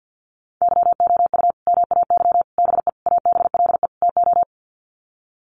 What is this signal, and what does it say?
Morse code, heard as a keyed tone, about 50 wpm, 700 Hz. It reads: YOU MAY BE ABLE TO
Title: Sets of 5 Words Encore 50wpm